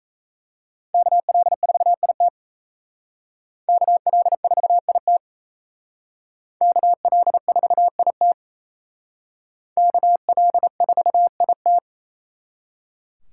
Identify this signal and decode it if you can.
Morse code, heard as a keyed tone, about 35 wpm, 700 Hz. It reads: KL4IT KL4IT KL4IT KL4IT
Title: Speed Racing - USA 2x2 Call Signs 28wpm